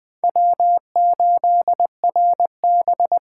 Morse code, heard as a keyed tone, about 20 wpm, 700 Hz. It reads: W8RB
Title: USA Call Signs - 8x Character Spacing - 20wpm